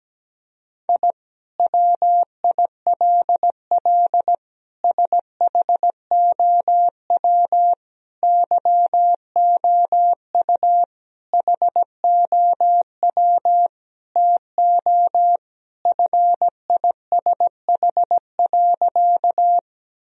Morse code, heard as a keyed tone, about 17 wpm, 700 Hz. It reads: I WILL SHOW YOU HOW TO FISH.
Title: Sentences from Top 1000 Words - Review Part 1 - 17wpm